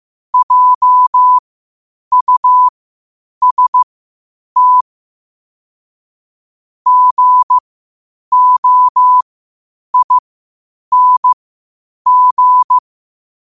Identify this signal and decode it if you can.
Morse code, heard as a keyed tone, about 15 wpm, spaced as for 8 wpm, 1000 Hz. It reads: JUST GOING
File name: qotdfiles / 8wpm